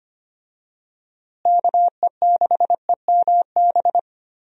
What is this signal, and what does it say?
Morse code, heard as a keyed tone, about 25 wpm, 700 Hz. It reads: KE6EMB
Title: USA Call Signs - 6 Characters - Repeated 2x - 25wpm